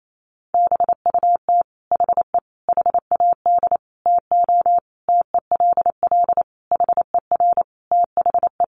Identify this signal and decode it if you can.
Morse code, heard as a keyed tone, about 28 wpm, 700 Hz. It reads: BUT HE HAD TO TELL HER THE
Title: Sentences from Top 800 Words 28wpm